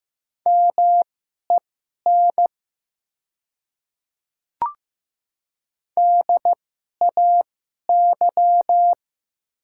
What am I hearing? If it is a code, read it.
Morse code, heard as a keyed tone, about 15 wpm, spaced as for 11 wpm, 700 Hz. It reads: MEN DAY
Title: 2x Character Spacing - Top 200 Words - 15wpm